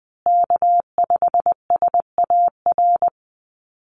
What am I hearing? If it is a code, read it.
Morse code, heard as a keyed tone, about 20 wpm, 700 Hz. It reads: K5SAR
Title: USA Call Signs Repeated 3x 20wpm